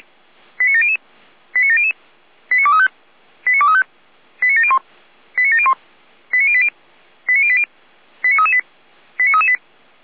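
Five-tone selective call (ZVEI), heard as quick runs of five tones, tones of about 70 ms. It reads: 89700 89700 89235 89235 89701 89701 89080 89080 89338 89338
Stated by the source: ZVEI five-tone selective call received on an SDR.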